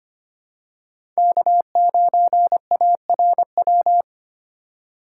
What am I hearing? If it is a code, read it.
Morse code, heard as a keyed tone, about 25 wpm, 700 Hz. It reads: K9ARW
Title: USA Call Signs - 4x Character Spacing - 25wpm